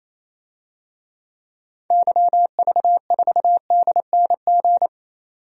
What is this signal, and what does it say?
Morse code, heard as a keyed tone, about 28 wpm, 700 Hz. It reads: YV4DNG
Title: DXCC-100 Call Signs - 6 Characters - 28wpm